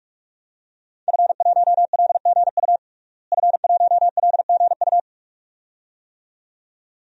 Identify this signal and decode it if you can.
Morse code, heard as a keyed tone, about 46 wpm, 700 Hz. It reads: F1LGU F1LGU
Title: DXCC-100 Call Signs - 5 Characters - Repeated 2x - 45wpm